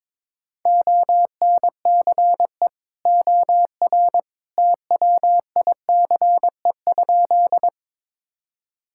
Morse code, heard as a keyed tone, about 21 wpm, 700 Hz. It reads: ONCE OR TWICE?
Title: Sentences from Top 1600 Words 22wpm